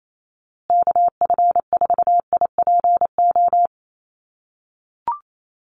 Morse code, heard as a keyed tone, about 28 wpm, 700 Hz. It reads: KF4IPO
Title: USA Call Signs Repeated 2x 28wpm